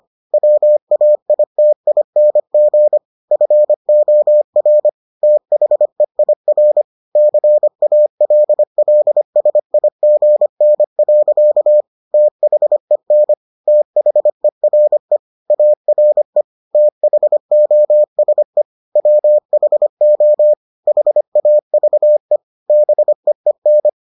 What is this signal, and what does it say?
Morse code, heard as a keyed tone, about 25 wpm, 600 Hz. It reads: WAITING FOR THEIR CALLSIGN. THEN THERE ARE THOSE WHO HAVE BEEN